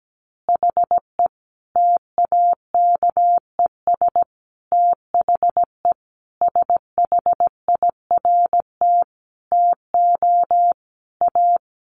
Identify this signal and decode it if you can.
Morse code, heard as a keyed tone, about 17 wpm, 700 Hz. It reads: HE TAKES THE SHIRT TO A
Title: Additional Easy Phrases 17wpm